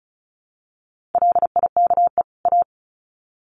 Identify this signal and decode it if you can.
Morse code, heard as a keyed tone, about 34 wpm, 700 Hz. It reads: LIKE A
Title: Sets of 2 Words 35wpm